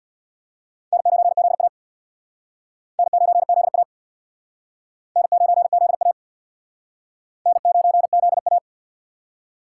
Morse code, heard as a keyed tone, about 61 wpm, 700 Hz. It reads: N9ZA N9ZA N9ZA N9ZA
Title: Speed Racing - USA 1x2 Call Signs 50wpm